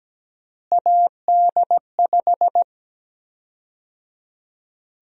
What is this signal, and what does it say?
Morse code, heard as a keyed tone, about 17 wpm, 700 Hz. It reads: AD5
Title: USA Call Sign Prefixes - Repeated 2x - 17wpm